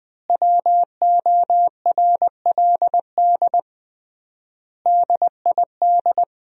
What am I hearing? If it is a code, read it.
Morse code, heard as a keyed tone, about 20 wpm, 700 Hz. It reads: WORLD DID
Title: 3x Word-Spacing - Sentences from Top 200 Words - 20wpm